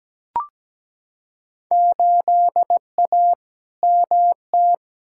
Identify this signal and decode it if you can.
Morse code, heard as a keyed tone, about 17 wpm, 700 Hz. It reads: 8A MT
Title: Field Day Exchanges Encore - 17wpm